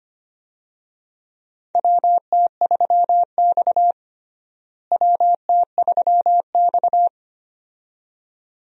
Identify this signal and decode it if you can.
Morse code, heard as a keyed tone, about 25 wpm, 700 Hz. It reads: WT3X WT3X
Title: USA Call Signs - 4 Characters - Repeated 2x - 25wpm